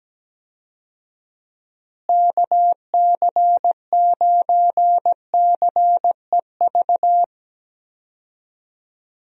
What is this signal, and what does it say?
Morse code, heard as a keyed tone, about 17 wpm, 700 Hz. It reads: KC9CEV
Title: USA Call Signs - 6 Characters - Repeated 2x - 17wpm